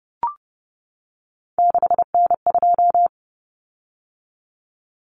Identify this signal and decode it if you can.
Morse code, heard as a keyed tone, about 29 wpm, 700 Hz. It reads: 6N2
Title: Common Cut Numbers - 3-Digit 30wpm